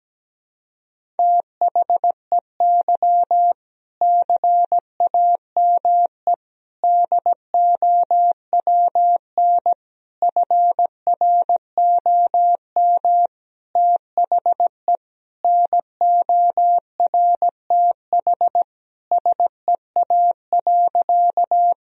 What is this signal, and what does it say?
Morse code, heard as a keyed tone, about 17 wpm, 700 Hz. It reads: THEY CAME DOWN FROM THE NORTH SEA.
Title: Sentences from Top 1000 Words - Review Part 1 - 17wpm